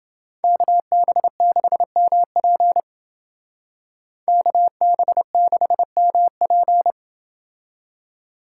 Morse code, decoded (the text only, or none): KB6MP KB6MP